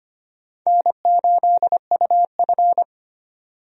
Morse code, heard as a keyed tone, about 25 wpm, 700 Hz. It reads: N8UF
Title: USA Call Signs - 4 Characters - 25wpm